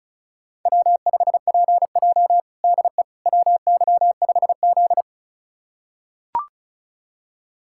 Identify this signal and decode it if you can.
Morse code, heard as a keyed tone, about 36 wpm, 700 Hz. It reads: W5PJ DE WY5Z
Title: USA Callsign DE Callsign - 4 Characters - 35wpm